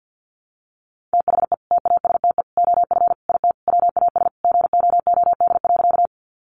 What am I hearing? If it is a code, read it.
Morse code, heard as a keyed tone, about 49 wpm, 700 Hz. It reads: THE TASTE OF IT WAS GOOD.
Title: Sentences from Top 1100 Words 50wpm